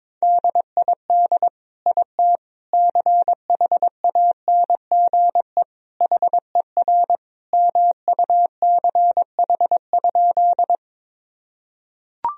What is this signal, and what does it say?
Morse code, heard as a keyed tone, about 22 wpm, 700 Hz. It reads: DID IT CHANGE HER MUCH?
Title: Sentences from Top 1000 Words - Review Part 1 - 22wpm